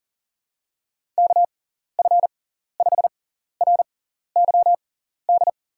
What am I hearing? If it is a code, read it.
Morse code, heard as a keyed tone, about 37 wpm, spaced as for 14 wpm, 700 Hz. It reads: KF5RYD
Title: USA Call Signs - 6 Characters - 6x Character Spacing - 40wpm